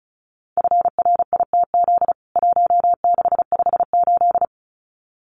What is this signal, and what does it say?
Morse code, heard as a keyed tone, about 34 wpm, 700 Hz. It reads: FRITZ 1658
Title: CWT Exchange 35wpm